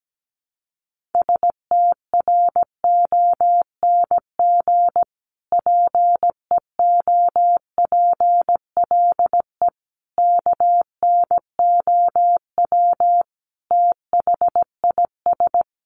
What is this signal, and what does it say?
Morse code, heard as a keyed tone, about 17 wpm, 700 Hz. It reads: STRONG PEOPLE KNOW THIS
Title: Sentences from Top 1500 Words 17wpm